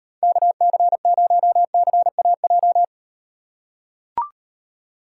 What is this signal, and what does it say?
Morse code, heard as a keyed tone, about 38 wpm, 700 Hz. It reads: KC0CAJ